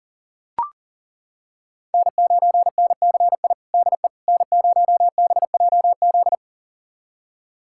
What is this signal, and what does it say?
Morse code, heard as a keyed tone, about 40 wpm, 700 Hz. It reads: N9NCI DE N0BJZ
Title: USA Callsign DE Callsign - 5 Characters - 40wpm